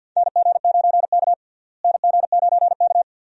Morse code, heard as a keyed tone, about 51 wpm, 700 Hz. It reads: NG9K NG9K
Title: USA Call Signs - 4 Characters - Repeated 2x - 50wpm